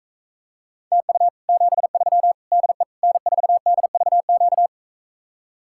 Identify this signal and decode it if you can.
Morse code, heard as a keyed tone, about 45 wpm, 700 Hz. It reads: TU 73 DE N4DVQ